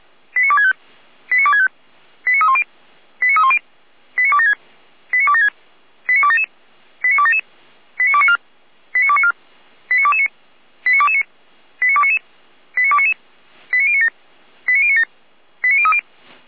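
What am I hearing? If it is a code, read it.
Five-tone selective call (ZVEI), heard as quick runs of five tones, tones of about 80 ms. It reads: 89265 89265 89310 89310 89276 89276 89277 89277 89294 89294 89229 89229 89299 89299 89097 89097 89030
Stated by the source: ZVEI five-tone selective call received on an SDR, from the feuerwehr-orlamuende.org server (Feuerwehr Orlamünde).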